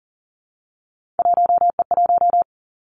Morse code, heard as a keyed tone, about 41 wpm, 700 Hz. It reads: 1E1
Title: Three Challenging Letter-Number Combinations 40wpm